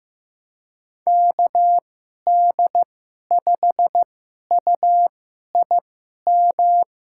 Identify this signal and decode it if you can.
Morse code, heard as a keyed tone, about 15 wpm, spaced as for 11 wpm, 700 Hz. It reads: KD5UIM
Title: USA Call Signs - 2x Character Spacing - 15wpm